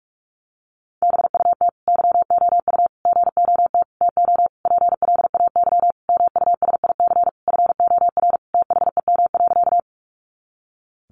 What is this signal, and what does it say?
Morse code, heard as a keyed tone, about 45 wpm, 700 Hz. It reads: BUT YOU GOT TO PLAY MUSIC FOR THEM.
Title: Sentences from Top 300 Words 45wpm